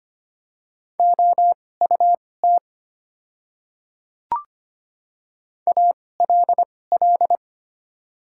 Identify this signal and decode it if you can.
Morse code, heard as a keyed tone, about 25 wpm, spaced as for 18 wpm, 690 Hz. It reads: OUT ALL
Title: 2x Character Spacing - Top 100 Words - 25wpm